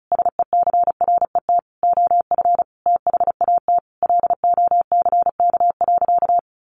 Morse code, heard as a keyed tone, about 34 wpm, 700 Hz. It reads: SECRET OF THAT LOCK.
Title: Sentences from Top 1300 Words 35wpm